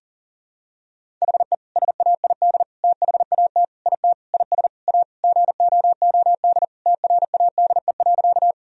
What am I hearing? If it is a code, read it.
Morse code, heard as a keyed tone, about 40 wpm, 700 Hz. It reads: HE SAID THAT IT IS A GOOD TRADE.
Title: Sentences from Top 1000 Words - Review Part 4 - 40wpm